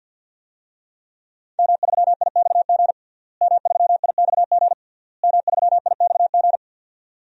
Morse code, heard as a keyed tone, about 52 wpm, 700 Hz. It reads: M3IXG M3IXG M3IXG
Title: DXCC-100 Call Signs - 5 Characters - Repeated 3x - 50wpm